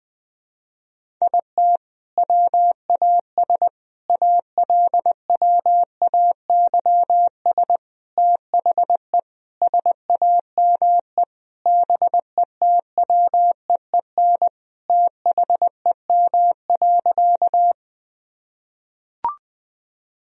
Morse code, heard as a keyed tone, about 20 wpm, 700 Hz. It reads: IT WAS ALWAYS THE SAME BETWEEN THEM.
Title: Sentences from Top 1000 Words - Review Part 1 - 20wpm